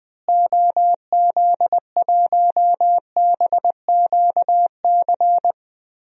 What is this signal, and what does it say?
Morse code, heard as a keyed tone, about 20 wpm, 700 Hz. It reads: OZ1BQC
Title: DXCC-100 Call Signs - 6 Characters - Repeated 2x - 20wpm